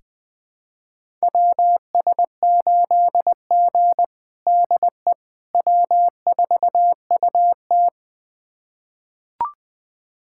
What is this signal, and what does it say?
Morse code, heard as a keyed tone, about 20 wpm, 700 Hz. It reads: WS8G DE W4UT
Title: USA Callsign DE Callsign - 4 Characters - 20wpm